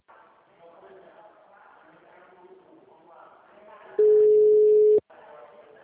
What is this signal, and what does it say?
Ringback tone (Europe, 425 Hz), with speech.
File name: out- -2015- - - .56914